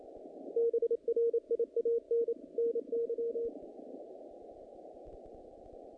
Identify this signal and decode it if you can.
Morse code, heard as a keyed tone, about 29 wpm, 470 Hz. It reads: BRIAN NY
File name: contests / naqp-cw-k5tr-op-k5pi- -2100